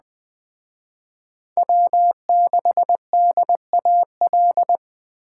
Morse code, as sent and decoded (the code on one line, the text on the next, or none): .-- -.... -.. .- .-..
W6DAL